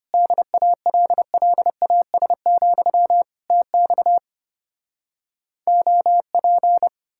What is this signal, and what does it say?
Morse code, heard as a keyed tone, about 33 wpm, 700 Hz. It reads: DALLAS, TX OP